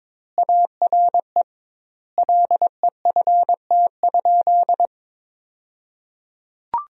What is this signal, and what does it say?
Morse code, heard as a keyed tone, about 22 wpm, 700 Hz. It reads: ARE LEFT?
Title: 2x Word-Spacing - Sentences from Top 300 Words - 22wpm